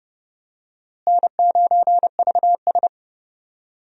Morse code, heard as a keyed tone, about 30 wpm, 700 Hz. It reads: N9VS
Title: USA Call Signs - 4 Characters - Repeated 2x - 30wpm